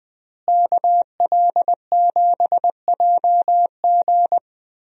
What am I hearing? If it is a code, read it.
Morse code, heard as a keyed tone, about 20 wpm, 700 Hz. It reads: KL7JG
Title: DXCC-100 Call Signs - 5 Characters - Repeated 2x - 20wpm